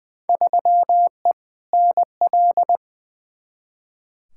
Morse code, heard as a keyed tone, about 20 wpm, 700 Hz. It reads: 3E NL
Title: Field Day Exchanges Encore - 20wpm